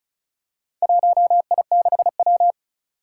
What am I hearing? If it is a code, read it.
Morse code, heard as a keyed tone, about 36 wpm, 700 Hz. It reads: 1I6W
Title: Four Challenging Letter-Number Combinations 35wpm